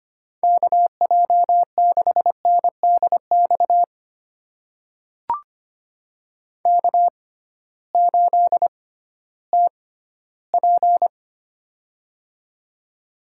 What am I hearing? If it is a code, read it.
Morse code, heard as a keyed tone, about 25 wpm, spaced as for 14 wpm, 700 Hz. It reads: KJ6NDX K8TP
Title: USA Call Signs - 6x Character Spacing - 25wpm